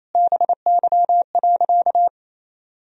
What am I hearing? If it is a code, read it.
Morse code, heard as a keyed tone, about 28 wpm, 700 Hz. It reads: BY.
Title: Sentences from Top 1000 Words - Review Part 3 - 28wpm